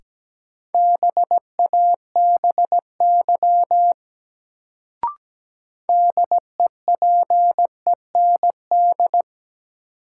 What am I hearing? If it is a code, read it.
Morse code, heard as a keyed tone, about 17 wpm, 700 Hz. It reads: BABY DEPEND
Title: Sentences from Top 1300 Words - New Word Drill - 17wpm